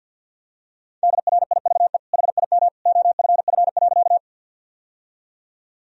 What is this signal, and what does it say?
Morse code, heard as a keyed tone, about 47 wpm, 700 Hz. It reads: DRIVE HIM OFF.
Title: Sentences from Top 1000 Words - Review Part 2 - 50wpm